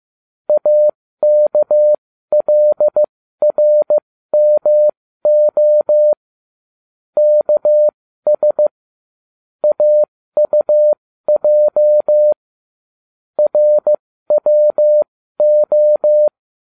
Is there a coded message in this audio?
Yes: Morse code from 0.5 to 16.3 s, heard as a keyed tone, about 15 wpm, spaced as for 13 wpm, 600 Hz. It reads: AKLRMO KS AUJ RWO